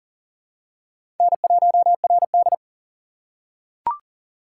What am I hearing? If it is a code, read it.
Morse code, heard as a keyed tone, about 40 wpm, 690 Hz. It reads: N1RD
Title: USA Call Signs - 4 Characters - 2x Character Spacing - 40wpm